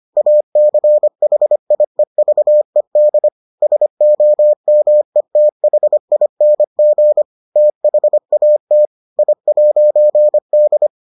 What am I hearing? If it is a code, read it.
Morse code, heard as a keyed tone, about 25 wpm, 600 Hz. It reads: ACHIEVED SOMETHING THAT I'D